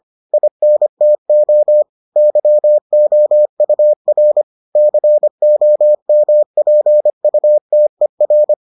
Morse code, heard as a keyed tone, about 25 wpm, 600 Hz. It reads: INTO YOUR COMPUTER